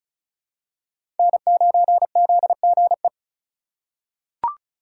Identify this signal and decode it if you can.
Morse code, heard as a keyed tone, about 36 wpm, 700 Hz. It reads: N9ZGE
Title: USA Call Signs - 5 Characters - 2x Character Spacing - 35wpm